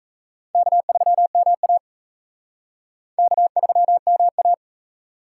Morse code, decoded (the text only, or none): K3MA K3MA